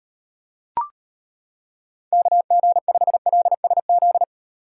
Morse code, heard as a keyed tone, about 31 wpm, 700 Hz. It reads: KG5LSZ
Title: Speed Racing - USA 2x3 Call Signs 25wpm